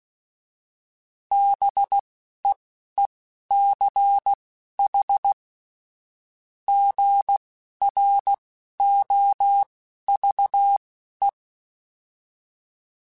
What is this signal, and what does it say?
Morse code, heard as a keyed tone, about 16 wpm, spaced as for 11 wpm, 800 Hz. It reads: BEECH GROVE
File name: fromSMs / FromSMs mp3s / Cities@10